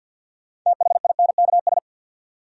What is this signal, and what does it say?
Morse code, heard as a keyed tone, about 51 wpm, 700 Hz. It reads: THINKS
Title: Sentences from Top 1000 Words - Review Part 6 - New Word Drill - 50wpm